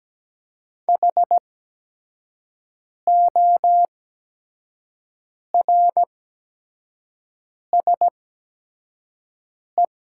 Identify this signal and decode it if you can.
Morse code, heard as a keyed tone, about 17 wpm, spaced as for 5 wpm, 700 Hz. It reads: HORSE
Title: 8x Character Spacing - Top 300 Words - 17wpm